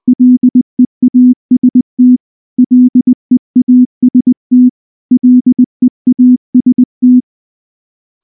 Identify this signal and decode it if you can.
Morse code, heard as a keyed tone, about 19 wpm, 250 Hz. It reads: LEAST LEAST LEAST
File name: morse / morse-training-20WPM-8